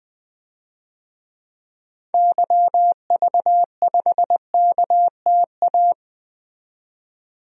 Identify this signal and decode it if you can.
Morse code, heard as a keyed tone, about 20 wpm, 700 Hz. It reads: YV5KTA
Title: DXCC-100 Call Signs - 6 Characters - 20wpm